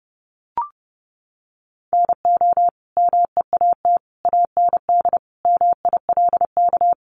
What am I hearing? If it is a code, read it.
Morse code, heard as a keyed tone, about 30 wpm, 700 Hz. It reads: NO MEAT AND MILK
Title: Sentences from Top 1500 Words 30wpm